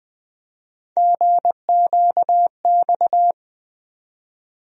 Morse code, heard as a keyed tone, about 20 wpm, 700 Hz. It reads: GQX
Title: USA Call Sign Postfixes - 3 Letters 20wpm